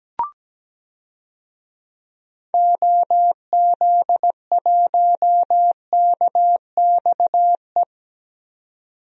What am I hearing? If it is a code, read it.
Morse code, heard as a keyed tone, about 17 wpm, 700 Hz. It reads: OZ1KXE
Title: DXCC-100 Call Signs - 6 Characters - 17wpm